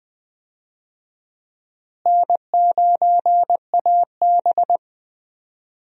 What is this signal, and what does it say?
Morse code, heard as a keyed tone, about 20 wpm, 700 Hz. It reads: N9AB
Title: USA Call Signs - 4 Characters - Repeated 2x - 20wpm